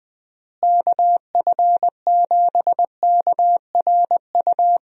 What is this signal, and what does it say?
Morse code, heard as a keyed tone, about 21 wpm, 700 Hz. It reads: KF7KRU
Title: USA Call Signs - 6 Characters - Repeated 3x - 20wpm